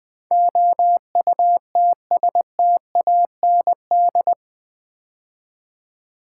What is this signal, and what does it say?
Morse code, heard as a keyed tone, about 20 wpm, 700 Hz. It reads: OUTSTAND
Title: Build A Word - 4 Letters At A Time - 11-Letter Words - 20wpm